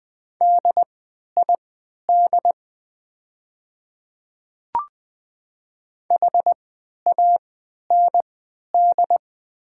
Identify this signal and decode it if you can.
Morse code, heard as a keyed tone, about 20 wpm, spaced as for 11 wpm, 700 Hz. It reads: DID HAND
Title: 3x Character Spacing - Top 200 Words - 20wpm